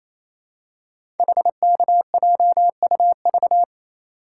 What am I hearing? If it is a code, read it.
Morse code, heard as a keyed tone, about 28 wpm, 700 Hz. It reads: HKJUV
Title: Code Groups - 5 Letters - 28wpm